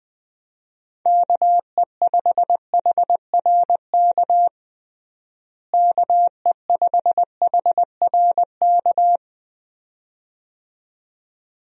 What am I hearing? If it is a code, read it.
Morse code, heard as a keyed tone, about 20 wpm, 700 Hz. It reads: KE5HRK KE5HRK
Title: USA Call Signs - 6 Characters - Repeated 2x - 20wpm